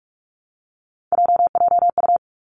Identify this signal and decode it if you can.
Morse code, heard as a keyed tone, about 46 wpm, 700 Hz. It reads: JJU
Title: Three Challenging Letter-Number Combinations 45wpm